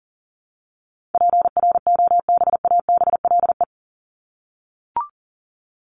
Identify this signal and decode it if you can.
Morse code, heard as a keyed tone, about 40 wpm, 700 Hz. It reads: PROBABLE
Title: LY Suffix - Top 100 Words with Base Words 40wpm